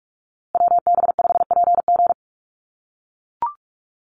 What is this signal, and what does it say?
Morse code, heard as a keyed tone, about 44 wpm, 700 Hz. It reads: WB5PG
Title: USA Call Signs Repeated 2x 45wpm